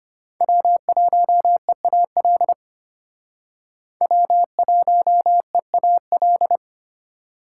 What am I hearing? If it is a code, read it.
Morse code, heard as a keyed tone, about 32 wpm, 700 Hz. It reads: W1EAL W1EAL